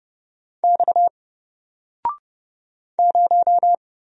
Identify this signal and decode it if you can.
Morse code, heard as a keyed tone, about 30 wpm, 700 Hz. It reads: X 0
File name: Single-Letter-Number-30wpm